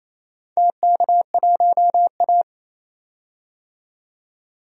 Morse code, heard as a keyed tone, about 28 wpm, 700 Hz. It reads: TK1A
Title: DXCC-100 Call Signs - 4 Characters - 28wpm